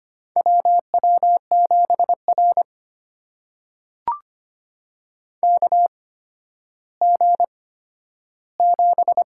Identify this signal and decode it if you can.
Morse code, heard as a keyed tone, about 25 wpm, spaced as for 12 wpm, 700 Hz. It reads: WW7R KG7
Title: USA Call Signs - 8x Character Spacing - 25wpm